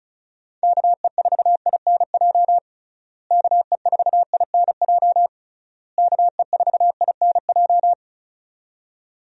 Morse code, heard as a keyed tone, about 34 wpm, 700 Hz. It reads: KE4INJ KE4INJ KE4INJ
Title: USA Call Signs - 6 Characters - Repeated 3x - 35wpm